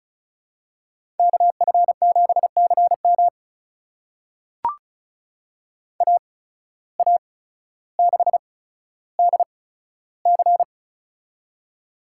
Morse code, heard as a keyed tone, about 34 wpm, spaced as for 15 wpm, 700 Hz. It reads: KF7CM AA6DC
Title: USA Call Signs - 5 Characters - 8x Character Spacing - 35wpm